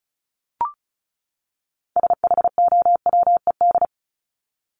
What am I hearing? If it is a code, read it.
Morse code, heard as a keyed tone, about 34 wpm, 700 Hz. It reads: SHOWED
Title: ED Suffix - Top 100 Words with Base Words 35wpm